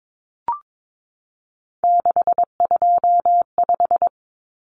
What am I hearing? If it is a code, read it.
Morse code, heard as a keyed tone, about 22 wpm, 700 Hz. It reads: 625
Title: Numbers - 3-Digit 22wpm